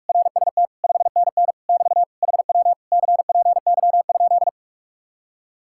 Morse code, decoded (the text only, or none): RST 5NN <BT> HW CPY?